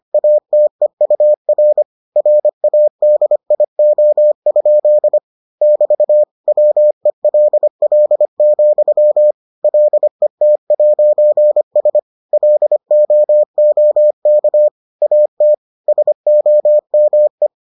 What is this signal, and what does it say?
Morse code, heard as a keyed tone, about 25 wpm, 600 Hz. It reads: ATEUR RADIO? <BT> WELL, LET'S LOOK AT SOME